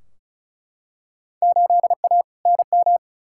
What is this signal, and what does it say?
Morse code, heard as a keyed tone, about 36 wpm, 700 Hz. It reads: 8A NM
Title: Field Day Exchanges Encore - 35wpm